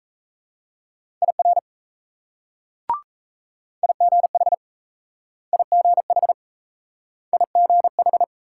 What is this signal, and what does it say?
Morse code, heard as a keyed tone, about 38 wpm, 700 Hz. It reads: IR IGH IGH IGH